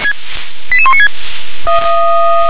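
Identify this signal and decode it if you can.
Five-tone selective call (ZVEI), heard as quick runs of five tones, tones of about 70 ms. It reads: 88196
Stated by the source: ZVEI five-tone selective call received on an SDR.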